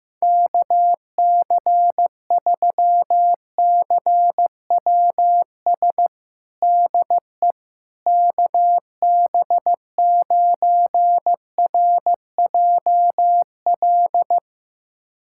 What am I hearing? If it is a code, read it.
Morse code, heard as a keyed tone, about 15 wpm, 700 Hz. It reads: KC3CWS DE KB9RJL
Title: USA Callsign DE Callsign - 6 Characters - 15wpm